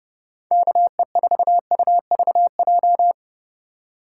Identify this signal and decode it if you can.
Morse code, heard as a keyed tone, about 30 wpm, 700 Hz. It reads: KE4UVJ
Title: USA Call Signs - 6 Characters - 4x Character Spacing - 30wpm